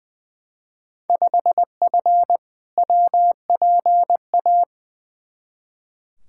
Morse code, heard as a keyed tone, about 20 wpm, 700 Hz. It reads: 5F WPA
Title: Field Day Exchanges Encore - 20wpm